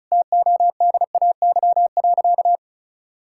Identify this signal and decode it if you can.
Morse code, heard as a keyed tone, about 34 wpm, 700 Hz. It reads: TODAY.
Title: Sentences from Top 1600 Words 35wpm